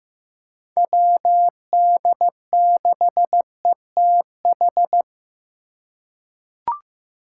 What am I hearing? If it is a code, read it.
Morse code, heard as a keyed tone, about 15 wpm, 700 Hz. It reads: WD6ETH
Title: USA Call Signs - 8x Character Spacing - 15wpm